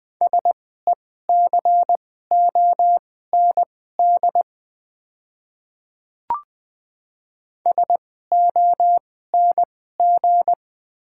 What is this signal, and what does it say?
Morse code, heard as a keyed tone, about 20 wpm, spaced as for 14 wpm, 690 Hz. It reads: SECOND SONG
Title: 2x Character Spacing - Top 500 Words - 20wpm